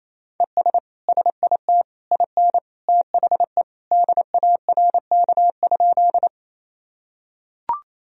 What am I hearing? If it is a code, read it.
Morse code, heard as a keyed tone, about 28 wpm, 700 Hz. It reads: ES SIT IN THE DARK?
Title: Sentences from Top 500 Words and Abbreviations 28wpm